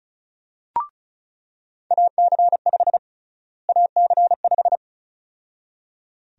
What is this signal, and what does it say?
Morse code, heard as a keyed tone, about 34 wpm, 700 Hz. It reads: AC5 AC5
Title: USA Call Sign Prefixes - Repeated 2x - 35wpm